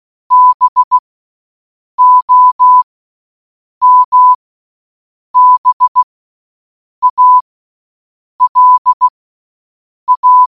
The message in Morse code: BOMBALA